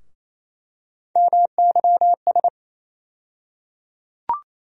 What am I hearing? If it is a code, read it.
Morse code, heard as a keyed tone, about 27 wpm, 700 Hz. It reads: MYS
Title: USA Call Sign Postfixes - 3 Letters 28wpm